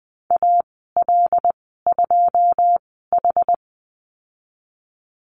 Morse code, heard as a keyed tone, about 20 wpm, spaced as for 15 wpm, 700 Hz. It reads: AL2H
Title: 193 Lesson 31g - Letter v - Call Signs 20wpm